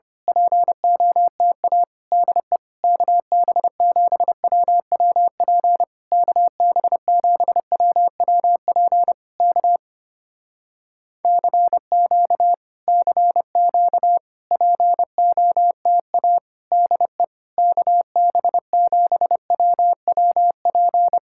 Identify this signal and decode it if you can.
Morse code, heard as a keyed tone, about 24 wpm, 700 Hz. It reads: POTA DE KB7WWP KB7WWP K CQ CQ POTA DE KB7WWP
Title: Speed-Racing Common QSO Phrases - 25wpm